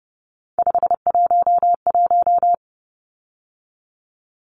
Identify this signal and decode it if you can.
Morse code, heard as a keyed tone, about 30 wpm, 700 Hz. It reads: 511
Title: Numbers - 3-Digit 30wpm